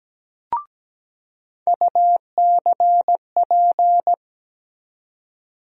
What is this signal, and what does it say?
Morse code, heard as a keyed tone, about 17 wpm, 700 Hz. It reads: UCP
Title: USA Call Sign Postfixes - 3 Letters 17wpm